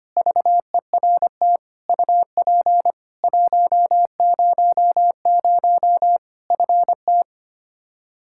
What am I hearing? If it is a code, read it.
Morse code, heard as a keyed tone, about 25 wpm, 700 Hz. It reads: VERT UP 100 FT
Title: Practice QSO - Exchange 2 - Info - 25wpm